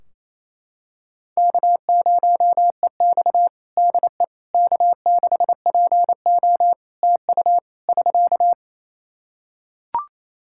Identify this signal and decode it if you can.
Morse code, heard as a keyed tone, about 28 wpm, 700 Hz. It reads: K0EX DE K6PO TU <SK>
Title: Practice QSO - Exchange 4 - Ending - 28wpm